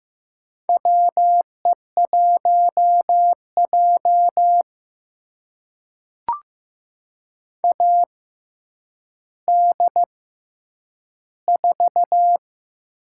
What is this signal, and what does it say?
Morse code, heard as a keyed tone, about 15 wpm, spaced as for 9 wpm, 700 Hz. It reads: WE1J AD4
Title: USA Call Signs - 4 Characters - 6x Character Spacing - 15wpm